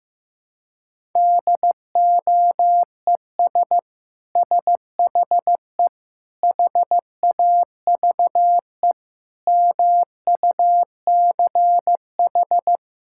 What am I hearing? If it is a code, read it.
Morse code, heard as a keyed tone, about 15 wpm, 700 Hz. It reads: DOES SHE HAVE MUCH
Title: Sentences from Top 1000 Words - Review Part 6 - 15wpm